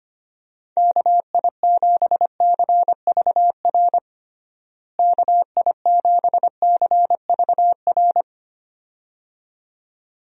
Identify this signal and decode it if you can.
Morse code, heard as a keyed tone, about 25 wpm, 700 Hz. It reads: KI7CVR KI7CVR
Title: USA Call Signs - 6 Characters - Repeated 2x - 25wpm